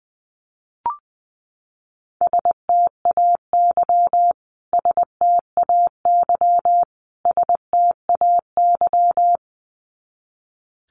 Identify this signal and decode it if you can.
Morse code, heard as a keyed tone, about 20 wpm, 700 Hz. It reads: STAY STAY STAY
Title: Top 500 Words Repeated 20wpm